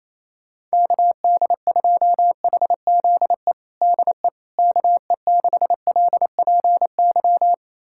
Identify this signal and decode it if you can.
Morse code, heard as a keyed tone, about 27 wpm, 700 Hz. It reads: KD2HZE DE KE6LPY
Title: USA Callsign DE Callsign - 6 Characters - 28wpm